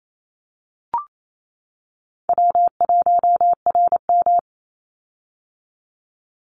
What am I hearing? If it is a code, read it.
Morse code, heard as a keyed tone, about 28 wpm, 700 Hz. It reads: W1RM
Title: Easy Call Signs - 28wpm